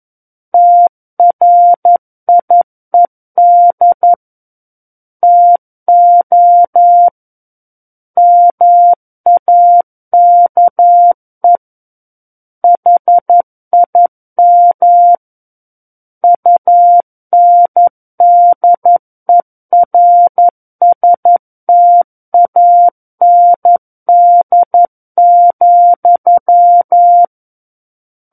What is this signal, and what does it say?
Morse code, heard as a keyed tone, about 11 wpm, 700 Hz. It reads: TRIED TO MAKE HIM UNDERSTAND,
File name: Book One / 02-Chapter 2 - 11 WPM